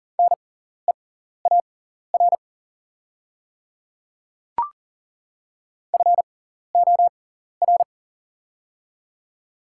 Morse code, heard as a keyed tone, about 40 wpm, spaced as for 14 wpm, 700 Hz. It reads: NEAR FOR
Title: 6x Character Spacing - Top 200 Words - 40wpm